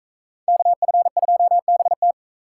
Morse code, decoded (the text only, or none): KF2BT